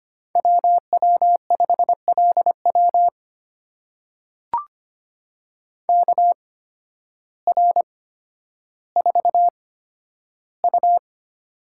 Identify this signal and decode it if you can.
Morse code, heard as a keyed tone, about 25 wpm, spaced as for 12 wpm, 700 Hz. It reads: WW5LW KR4U
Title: USA Call Signs - 8x Character Spacing - 25wpm